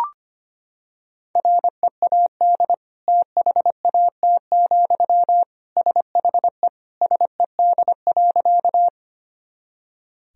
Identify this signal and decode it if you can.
Morse code, heard as a keyed tone, about 25 wpm, 700 Hz. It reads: READ THAT, SHE SED.
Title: Sentences from Top 200 Words and Abbreviations 25wpm